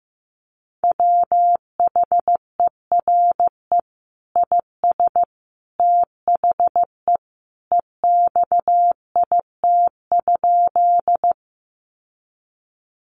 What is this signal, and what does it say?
Morse code, heard as a keyed tone, about 15 wpm, 700 Hz. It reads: WHERE IS THE EXIT?
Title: Sentences from Top 1100 Words 15wpm